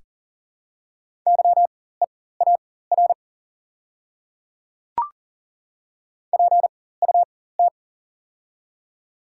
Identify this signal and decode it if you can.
Morse code, heard as a keyed tone, about 40 wpm, spaced as for 19 wpm, 690 Hz. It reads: YEAR PUT
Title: 4x Character Spacing - Top 300 Words - 40wpm